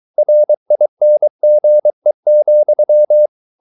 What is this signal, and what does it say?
Morse code, heard as a keyed tone, about 23 wpm, 600 Hz. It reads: RINGE,
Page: Ferie 23 wpm